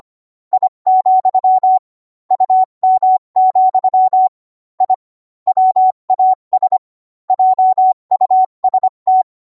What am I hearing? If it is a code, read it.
Morse code, heard as a keyed tone, about 25 wpm, 750 Hz. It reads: I, UM, I WAS JUST